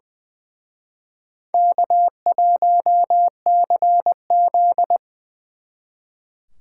Morse code, heard as a keyed tone, about 24 wpm, 700 Hz. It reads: K1CZ
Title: Speed Racing - USA 1x2 Call Signs 20wpm